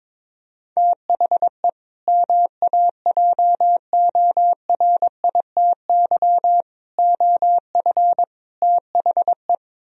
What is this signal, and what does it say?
Morse code, heard as a keyed tone, about 23 wpm, 700 Hz. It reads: THE MAJORITY OF THE
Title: Speed-Racing - Sets of 4 Words 22wpm